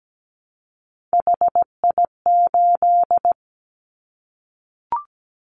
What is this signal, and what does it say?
Morse code, heard as a keyed tone, about 17 wpm, 700 Hz. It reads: HI8
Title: Callsign Prefixes for Top 100 DXCC - 17wpm